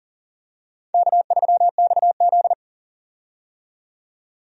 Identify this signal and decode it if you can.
Morse code, heard as a keyed tone, about 37 wpm, 700 Hz. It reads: K3XZ
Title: USA Call Signs - 4 Characters - 40wpm